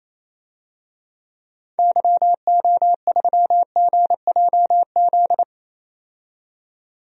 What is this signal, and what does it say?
Morse code, heard as a keyed tone, about 28 wpm, 700 Hz. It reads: YO3GJZ
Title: DXCC-100 Call Signs - 6 Characters - 28wpm